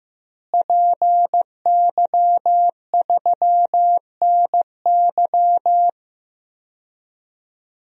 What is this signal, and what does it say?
Morse code, heard as a keyed tone, about 16 wpm, 700 Hz. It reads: PY3NY
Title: DXCC-100 Call Signs - 5 Characters - Repeated 2x - 15wpm